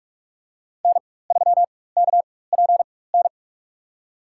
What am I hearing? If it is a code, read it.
Morse code, heard as a keyed tone, about 46 wpm, spaced as for 21 wpm, 700 Hz. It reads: N3KPN
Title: USA Call Signs - 4x Character Spacing - 45wpm